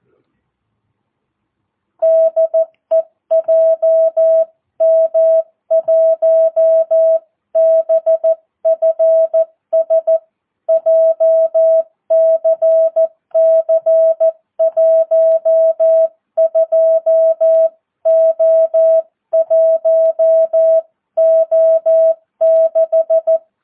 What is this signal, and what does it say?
Morse code, heard as a keyed tone, about 14 wpm, 650 Hz. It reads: DE JM1BFS JCC 12O1O6
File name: jh8cbh / 2022cw